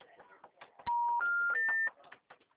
Special information tone.
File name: out- -2012